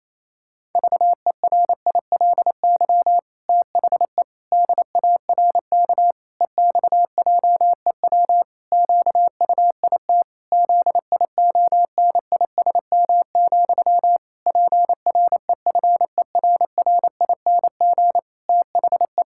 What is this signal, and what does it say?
Morse code, heard as a keyed tone, about 28 wpm, 700 Hz. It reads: VERILY THE DARK EXJEW QUIT ZIONISM, PREFERRING THE
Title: Pangram Challenge v2 - 28wpm